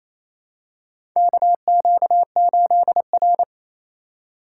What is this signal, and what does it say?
Morse code, heard as a keyed tone, about 28 wpm, 700 Hz. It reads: KQ8R
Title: USA Call Signs - 4x Character Spacing - 28wpm